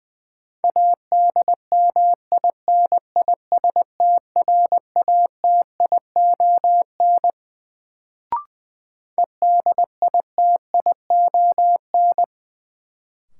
Morse code, heard as a keyed tone, about 21 wpm, 700 Hz. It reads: ADMINISTRATION EDITION
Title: Sets of 5 Words - New Word Drill - 20wpm